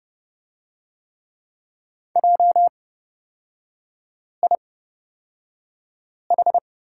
Morse code, heard as a keyed tone, about 30 wpm, spaced as for 5 wpm, 700 Hz. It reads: JIH
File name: Lesson 37e - Letter x - Three Characters 30wpm